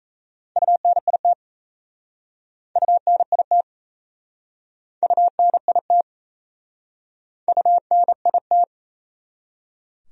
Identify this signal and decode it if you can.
Morse code, heard as a keyed tone, about 39 wpm, 700 Hz. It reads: UNIT UNIT UNIT UNIT